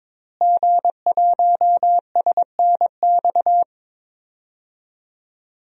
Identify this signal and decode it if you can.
Morse code, heard as a keyed tone, about 22 wpm, 700 Hz. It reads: G1SNX
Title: DXCC-100 Call Signs - 5 Characters - Repeated 3x - 22wpm